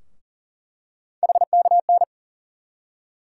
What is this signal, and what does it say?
Morse code, heard as a keyed tone, about 40 wpm, 700 Hz. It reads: HKN
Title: USA Call Sign Postfixes - 3 Letters 40wpm